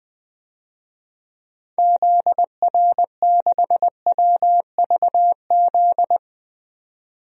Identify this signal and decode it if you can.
Morse code, heard as a keyed tone, about 20 wpm, 700 Hz. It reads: ZR6WVZ
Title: DXCC-100 Call Signs - 6 Characters - 20wpm